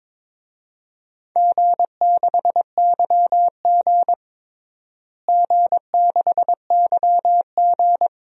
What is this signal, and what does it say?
Morse code, heard as a keyed tone, about 22 wpm, 700 Hz. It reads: G6YG G6YG
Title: DXCC-100 Call Signs - 4 Characters - Repeated 2x - 22wpm